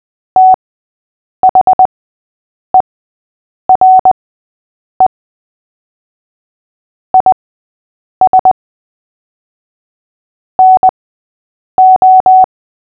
Morse code, heard as a keyed tone, about 20 wpm, spaced as for 8 wpm, 750 Hz. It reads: THERE IS NO
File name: cw / fists / Short-Story-8-8wpm